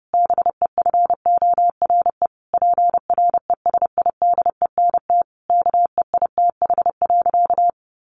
Morse code, heard as a keyed tone, about 30 wpm, 700 Hz. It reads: BEFORE PRESIDENT KEITH.
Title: A Sound Of Thunder 30wpm